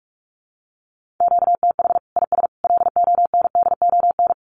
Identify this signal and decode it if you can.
Morse code, heard as a keyed tone, about 44 wpm, 700 Hz. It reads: QTH IS LONDON
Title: Common QSO Phrases 45wpm